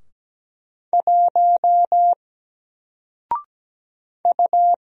Morse code, heard as a keyed tone, about 17 wpm, 700 Hz. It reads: 1 U
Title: Single Letter-Number 17wpm